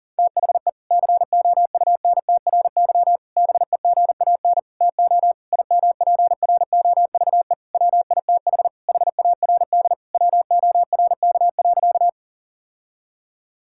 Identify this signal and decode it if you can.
Morse code, heard as a keyed tone, about 40 wpm, 700 Hz. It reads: THE COUNTRY BEGAN TO IMPROVE WITH HARD WORK.
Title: Sentences from Top 1000 Words - Review Part 5 - 40wpm